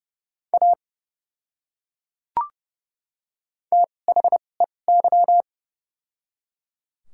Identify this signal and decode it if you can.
Morse code, heard as a keyed tone, about 30 wpm, spaced as for 22 wpm, 700 Hz. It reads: A THEY
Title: 2x Character Spacing - Top 100 Words - 30wpm